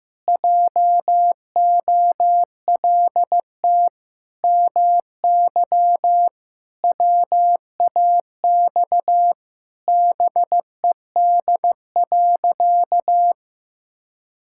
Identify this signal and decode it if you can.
Morse code, heard as a keyed tone, about 14 wpm, 700 Hz. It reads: JOLT MY WAX BED.
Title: Pangram Challenge v2 - 15wpm